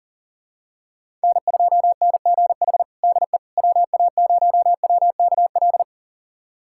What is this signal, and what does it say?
Morse code, heard as a keyed tone, about 40 wpm, 700 Hz. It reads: N2NGH DE WA0WKL
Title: USA Callsign DE Callsign 40wpm